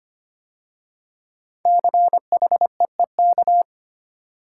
Morse code, heard as a keyed tone, about 25 wpm, 700 Hz. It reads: CHEEK
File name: 5 Letter Words - First 3 Letters Given - 25wpm